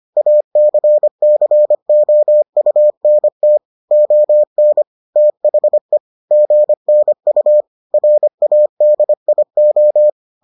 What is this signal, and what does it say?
Morse code, heard as a keyed tone, about 25 wpm, 600 Hz. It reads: ACCOUNT ON THE GNU RADIO